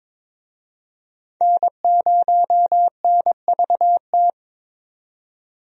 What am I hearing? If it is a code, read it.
Morse code, heard as a keyed tone, about 22 wpm, 700 Hz. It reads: N0NVT
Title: USA Call Signs - 5 Characters - 22wpm